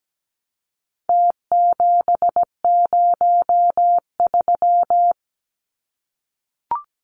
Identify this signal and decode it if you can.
Morse code, heard as a keyed tone, about 17 wpm, 700 Hz. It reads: T703
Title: Numbers - Contest Style - 4-Digit 17wpm